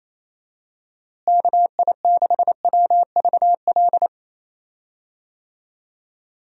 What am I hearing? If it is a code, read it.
Morse code, heard as a keyed tone, about 28 wpm, 700 Hz. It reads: KI6WVL
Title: USA Call Signs - 6 Characters - Repeated 3x - 28wpm